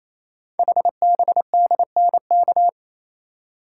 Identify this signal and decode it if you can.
Morse code, heard as a keyed tone, about 27 wpm, 700 Hz. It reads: HBDNK
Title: Code Groups - 5 Letters - 28wpm